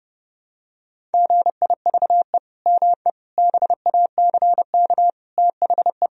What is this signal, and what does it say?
Morse code, heard as a keyed tone, about 30 wpm, 700 Hz. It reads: GIVE ME BACK THE
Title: Sentences from Top 1000 Words - Review Part 4 - 30wpm